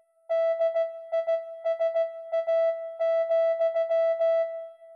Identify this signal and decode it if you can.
Morse code, heard as a keyed tone, about 16 wpm, 660 Hz. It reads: DISA,